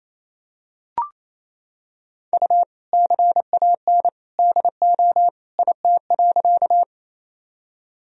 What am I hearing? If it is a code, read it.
Morse code, heard as a keyed tone, about 28 wpm, 700 Hz. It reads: U CAN DO IT.
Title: Sentences from Top 100 Words and Abbreviations 28wpm